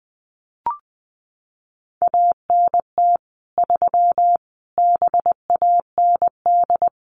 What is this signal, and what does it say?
Morse code, heard as a keyed tone, about 20 wpm, 700 Hz. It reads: ANT 3 BAND
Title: Common QSO Phrases 20wpm